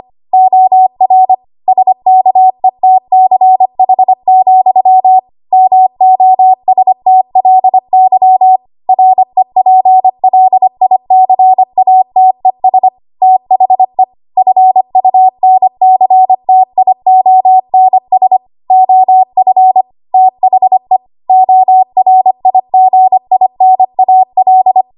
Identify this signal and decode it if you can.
Morse code, heard as a keyed tone, about 25 wpm, 750 Hz. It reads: OR SKETCH, MOSTLY REPLICATES THE FUNCTIONS OF THE ORIGINAL
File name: Archive / 25 WPM / 25WPM